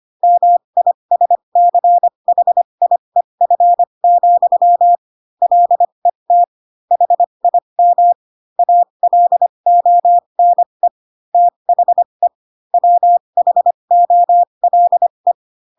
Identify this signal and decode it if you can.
Morse code, heard as a keyed tone, about 25 wpm, spaced as for 22 wpm, 700 Hz. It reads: MISCHIEF, LET HIM ALONE THE WHOLE
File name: Pinocchio / Pinocchio ch8 18 25 ik1hge